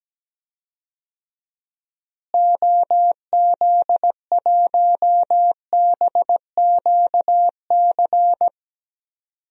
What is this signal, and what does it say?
Morse code, heard as a keyed tone, about 17 wpm, 700 Hz. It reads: OZ1BQC
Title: DXCC-100 Call Signs - 6 Characters - Repeated 2x - 17wpm